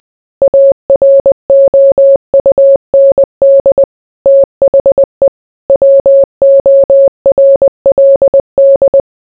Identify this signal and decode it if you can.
Morse code, heard as a keyed tone, about 20 wpm, 550 Hz. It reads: AROUND THE WORLD